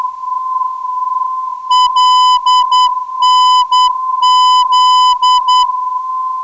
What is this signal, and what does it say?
Morse code, heard as a keyed tone, about 10 wpm, 3000 Hz, with noise. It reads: LNZ
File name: ndb zvok / LV - 327kHz